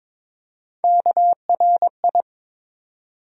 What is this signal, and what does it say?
Morse code, heard as a keyed tone, about 22 wpm, 700 Hz. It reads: KRI
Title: USA Call Sign Postfixes - 3 Letters 22wpm